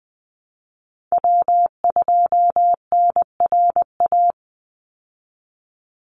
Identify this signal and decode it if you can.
Morse code, heard as a keyed tone, about 20 wpm, 700 Hz. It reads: W2NRA
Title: Call Signs Repeated 3x - 20wpm